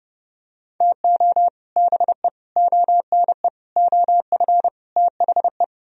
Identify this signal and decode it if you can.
Morse code, heard as a keyed tone, about 31 wpm, 700 Hz. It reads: TO BE ONE OF THE